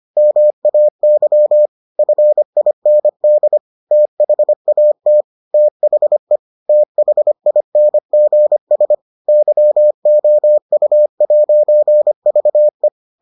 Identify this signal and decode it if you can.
Morse code, heard as a keyed tone, about 25 wpm, 600 Hz. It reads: MAY FIND THAT THE THINGS YOU'VE